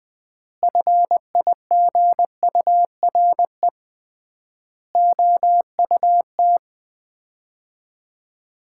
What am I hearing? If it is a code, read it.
Morse code, heard as a keyed tone, about 21 wpm, 700 Hz. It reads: FIGURE OUT
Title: 3x Word-Spacing - Sets of 4 Words - 20wpm